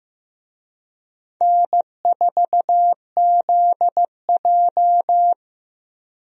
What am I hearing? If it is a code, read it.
Morse code, heard as a keyed tone, about 15 wpm, 700 Hz. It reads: N4ZJ
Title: USA Call Signs - 4 Characters - 15wpm